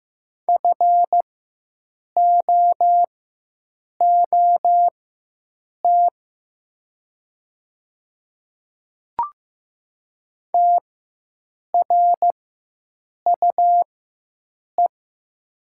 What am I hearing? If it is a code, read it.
Morse code, heard as a keyed tone, about 15 wpm, spaced as for 7 wpm, 700 Hz. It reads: FOOT TRUE
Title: 4x Character Spacing - Top 500 Words - 15wpm